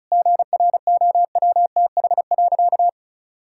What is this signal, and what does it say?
Morse code, heard as a keyed tone, about 34 wpm, 690 Hz. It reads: GROWTH.
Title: Sentences from Top 1000 Words - Review Part 5 - 35wpm